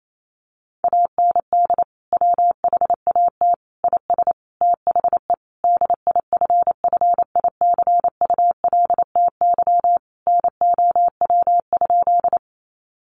Morse code, heard as a keyed tone, about 28 wpm, 700 Hz. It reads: AND WHAT IS THE DIFFICULTY NOW?
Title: Sentences from Top 1500 Words 28wpm